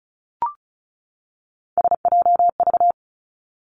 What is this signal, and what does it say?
Morse code, heard as a keyed tone, about 36 wpm, 700 Hz. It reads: SJV
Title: ARRL Sections 35wpm